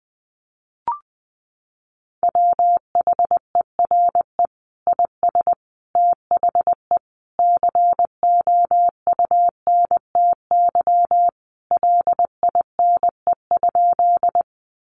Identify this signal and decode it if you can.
Morse code, heard as a keyed tone, about 20 wpm, 700 Hz. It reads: WHERE IS THE COUNTY LINE?
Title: Sentences from Top 1200 Words 20wpm